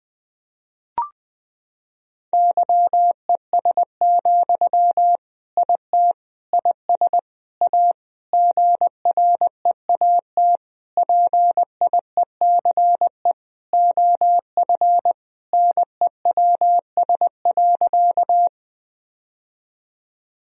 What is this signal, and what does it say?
Morse code, heard as a keyed tone, about 20 wpm, 700 Hz. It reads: YES, IT IS A GREAT PIECE OF NEWS.
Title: Sentences from Top 1000 Words - Review Part 2 - 20wpm